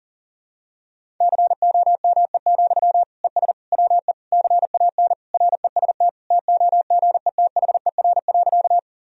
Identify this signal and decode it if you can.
Morse code, heard as a keyed tone, about 40 wpm, 700 Hz. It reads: COME, ES WE CAN REST TOGETHER.
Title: Sentences from Top 500 Words and Abbreviations 40wpm